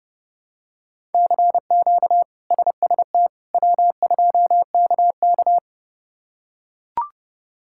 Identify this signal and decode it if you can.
Morse code, heard as a keyed tone, about 31 wpm, 700 Hz. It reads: CQ SST W2KK
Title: K1USN SST QSO Practice - 30wpm